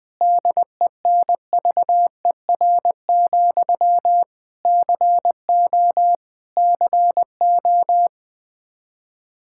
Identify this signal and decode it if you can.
Morse code, heard as a keyed tone, about 21 wpm, 700 Hz. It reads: DENVER, CO CO
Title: Practice QSO - Exchange 1 - Beginning - 20wpm